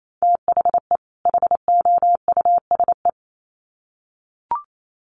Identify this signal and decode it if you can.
Morse code, heard as a keyed tone, about 28 wpm, 700 Hz. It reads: THE HOUSE
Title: Sets of 2 Words 28wpm